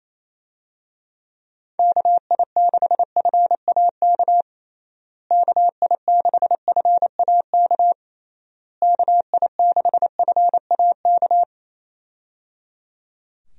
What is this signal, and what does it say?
Morse code, heard as a keyed tone, about 28 wpm, 700 Hz. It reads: KI6FAK KI6FAK KI6FAK
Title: USA Call Signs - 6 Characters - Repeated 3x - 28wpm